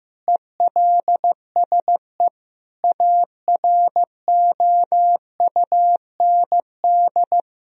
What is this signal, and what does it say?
Morse code, heard as a keyed tone, about 15 wpm, 700 Hz. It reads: ELSE AROUND
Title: Sentences from Top 1000 Words - Review Part 3 - 15wpm